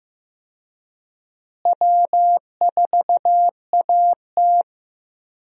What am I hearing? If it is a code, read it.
Morse code, heard as a keyed tone, about 15 wpm, 700 Hz. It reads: W4AT
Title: USA Call Signs - 4 Characters - Repeated 2x - 15wpm